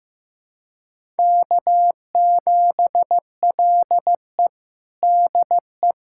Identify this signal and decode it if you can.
Morse code, heard as a keyed tone, about 15 wpm, 700 Hz. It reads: K7LE DE
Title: USA Callsign DE Callsign - 4 Characters - 15wpm